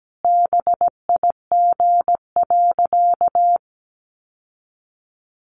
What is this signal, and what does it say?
Morse code, heard as a keyed tone, about 17 wpm, 700 Hz. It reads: BIG.
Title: Sentences from Top 600 Words 17wpm